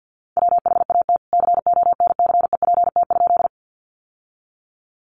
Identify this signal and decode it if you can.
Morse code, heard as a keyed tone, about 50 wpm, 700 Hz. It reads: WHAT CONCEPT?
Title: Sentences from Top 1100 Words 50wpm